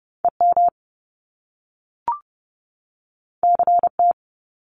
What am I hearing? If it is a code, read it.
Morse code, heard as a keyed tone, about 30 wpm, 700 Hz. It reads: EM CT
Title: Top Two Letter Combinations in Words 30wpm